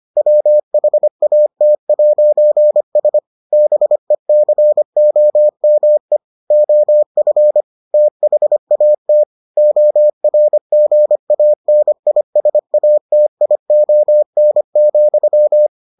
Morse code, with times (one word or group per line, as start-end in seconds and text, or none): 0.2-3.2 WHAT'S
3.5-6.2 BECOME
6.5-7.6 OF
7.9-9.2 THAT
9.6-15.7 ORGANISATION,